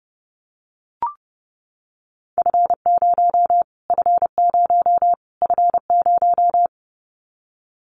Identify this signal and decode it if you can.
Morse code, heard as a keyed tone, about 30 wpm, 700 Hz. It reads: F0 F0 F0
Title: Callsign Prefixes for Top 25 DXCC - Repeated 3x - 30wpm